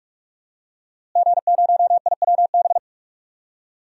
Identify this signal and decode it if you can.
Morse code, heard as a keyed tone, about 46 wpm, 700 Hz. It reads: G0IWB
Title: DXCC-100 Call Signs - 5 Characters - Repeated 2x - 45wpm